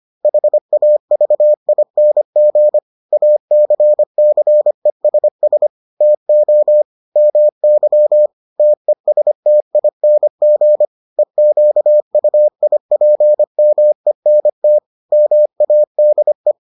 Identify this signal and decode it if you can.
Morse code, heard as a keyed tone, about 25 wpm, 600 Hz. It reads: HAVING ACCESS TO MY TESTING EQUIPMENT MADE